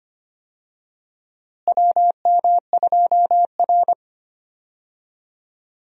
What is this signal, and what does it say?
Morse code, heard as a keyed tone, about 25 wpm, 700 Hz. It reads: WM2R
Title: USA Call Signs - 4 Characters - Repeated 3x - 25wpm